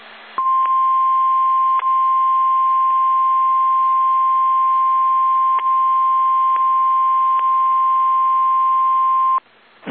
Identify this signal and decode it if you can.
Emergency Alert System (SAME) tones (the 1050 Hz attention tone). It is heard from 0.3 to 9.5 s.
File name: eas / KSDZKDJL / KSDZKDJL